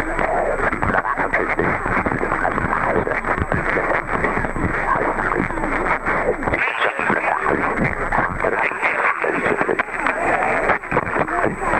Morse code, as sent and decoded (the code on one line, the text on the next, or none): none